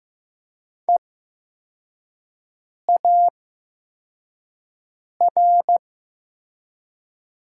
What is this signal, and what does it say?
Morse code, heard as a keyed tone, about 15 wpm, spaced as for 4 wpm, 700 Hz. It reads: EAR
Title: 8x Character Spacing - Top 500 Words - 15wpm